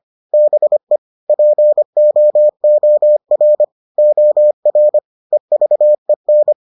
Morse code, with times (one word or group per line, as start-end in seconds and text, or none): 0.3-1.0 BE
1.3-3.6 POOR
4.0-5.0 OR
5.3-6.5 EVEN